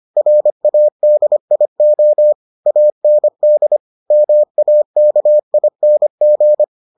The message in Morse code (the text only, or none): RADIO AND MAKING